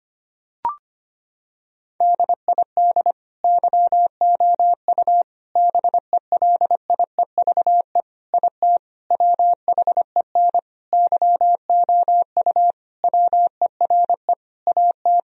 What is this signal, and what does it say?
Morse code, heard as a keyed tone, about 25 wpm, 700 Hz. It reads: DID YOU BELIEVE IT WHEN YOU WERE AT
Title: Sentences from Top 1000 Words - Review Part 1 - 25wpm